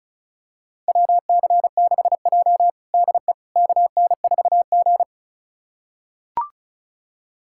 Morse code, heard as a keyed tone, about 34 wpm, 690 Hz. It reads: WC6J DE KN4G
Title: USA Callsign DE Callsign - 4 Characters - 35wpm